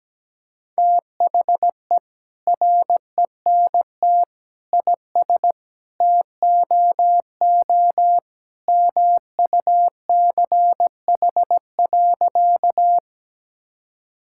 Morse code, heard as a keyed tone, about 17 wpm, 700 Hz. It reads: THE RENT IS TOO MUCH.
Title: Sentences from Top 1000 Words - Review Part 5 - 17wpm